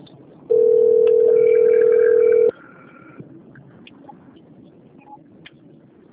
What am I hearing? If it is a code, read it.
Ringback tone (North America), 440 + 480 Hz.